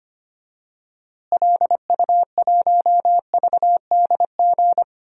Morse code, heard as a keyed tone, about 25 wpm, 700 Hz. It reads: LU1VDG
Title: DXCC-100 Call Signs - 6 Characters - Repeated 2x - 25wpm